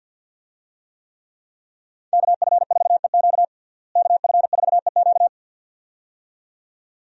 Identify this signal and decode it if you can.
Morse code, heard as a keyed tone, about 52 wpm, 700 Hz. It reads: KF4EQ KF4EQ
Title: USA Call Signs - 5 Characters - Repeated 2x - 50wpm